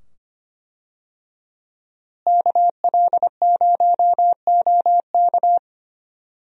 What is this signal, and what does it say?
Morse code, heard as a keyed tone, about 25 wpm, 700 Hz. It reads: KL0OK
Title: USA Call Signs - 5 Characters - Repeated 2x - 25wpm